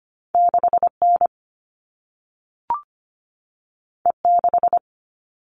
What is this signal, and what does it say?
Morse code, heard as a keyed tone, about 25 wpm, 700 Hz. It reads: 6N E6
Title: All Cut Numbers - 2-Digit 25wpm